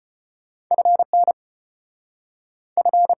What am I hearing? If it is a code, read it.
Morse code, heard as a keyed tone, about 33 wpm, 700 Hz. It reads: FN F